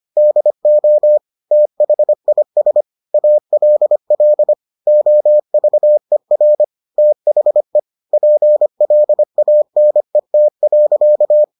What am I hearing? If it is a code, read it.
Morse code, heard as a keyed tone, about 25 wpm, 600 Hz. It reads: DO THIS ALL OVER THE PLANET.